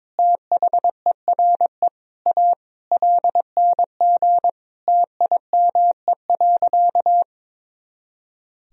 Morse code, heard as a keyed tone, about 22 wpm, 700 Hz. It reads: THERE A LNG TIME.
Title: Sentences from Top 400 Words and Abbreviations 22wpm